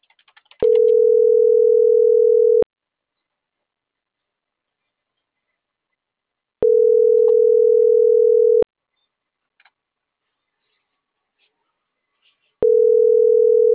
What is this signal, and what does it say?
Ringback tone (North America), 440 + 480 Hz.